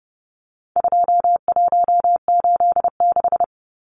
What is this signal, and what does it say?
Morse code, heard as a keyed tone, about 30 wpm, 700 Hz. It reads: 2186
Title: Numbers - 4-Digit 30wpm